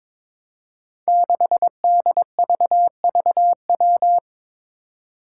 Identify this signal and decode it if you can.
Morse code, heard as a keyed tone, about 22 wpm, 700 Hz. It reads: 6DVVW
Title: Five Challenging Letter-Number Combinations 22wpm